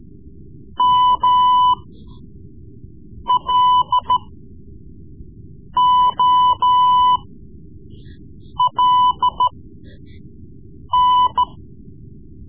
Morse code, heard as a keyed tone, about 13 wpm, spaced as for 5 wpm, 1000 Hz. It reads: MLOLN
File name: Morse Lessons / Morse Lessons / Tape-3B